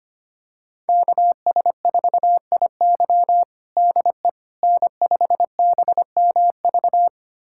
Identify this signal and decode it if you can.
Morse code, heard as a keyed tone, about 25 wpm, 700 Hz. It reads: KS4IY DE N5BMV
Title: USA Callsign DE Callsign - 5 Characters - 25wpm